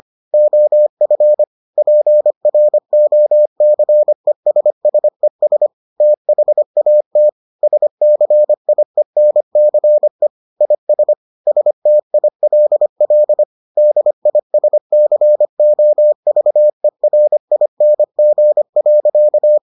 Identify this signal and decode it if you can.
Morse code, heard as a keyed tone, about 25 wpm, 600 Hz. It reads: OF PROCESSES THAT SCIENCE IS STILL DISCOVERING.